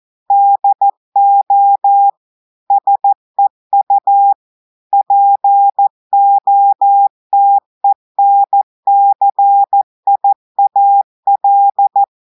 Morse code, heard as a keyed tone, about 14 wpm, 800 Hz. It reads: DO SEU POTENCIAL
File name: audios / portugais / portugais texte en clair 1 vitesse 14 mots minutes